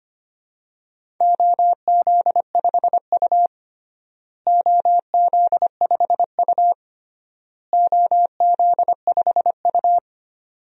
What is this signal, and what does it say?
Morse code, heard as a keyed tone, about 25 wpm, 700 Hz. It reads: OZ5U OZ5U OZ5U
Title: DXCC-100 Call Signs - 4 Characters - Repeated 3x - 25wpm